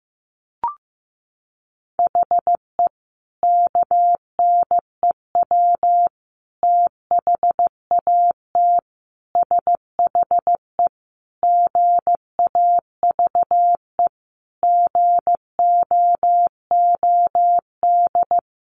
Morse code, heard as a keyed tone, about 15 wpm, 700 Hz. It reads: HE KNEW THAT SHE GAVE GOOD
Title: Sentences from Top 1100 Words 15wpm